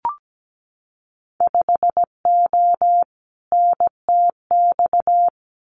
Morse code, heard as a keyed tone, about 17 wpm, 700 Hz. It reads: 5O NTX
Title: Winter Field Day Exchanges Encore 17wpm